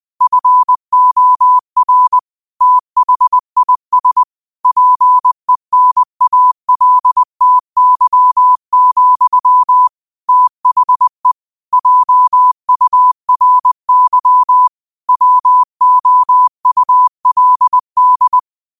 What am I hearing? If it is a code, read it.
Morse code, heard as a keyed tone, about 20 wpm, 1000 Hz. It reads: FOR THIS PENALTY, THE JURY WOULD